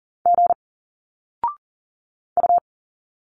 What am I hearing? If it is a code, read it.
Morse code, heard as a keyed tone, about 39 wpm, 700 Hz. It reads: G U
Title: Single Letters 40wpm